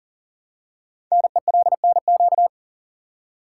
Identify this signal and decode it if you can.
Morse code, heard as a keyed tone, about 43 wpm, 700 Hz. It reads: NELNQ
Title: Code Groups - 5 Letters - 40wpm